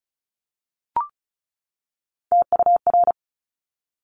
Morse code, heard as a keyed tone, about 36 wpm, 700 Hz. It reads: TUR
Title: Top Three Letter Combinations in Words 35wpm